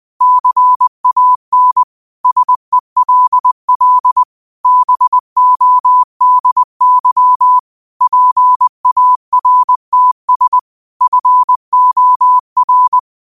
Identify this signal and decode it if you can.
Morse code, heard as a keyed tone, about 20 wpm, 1000 Hz. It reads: CAN SELL BODY PARTS FOR
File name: qotdfiles / 20wpm